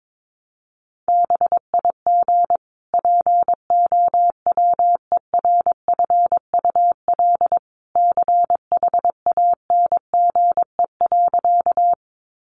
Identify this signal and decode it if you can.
Morse code, heard as a keyed tone, about 22 wpm, 700 Hz. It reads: BIG POWERFUL CHANGE.
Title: Sentences from Top 1300 Words 22wpm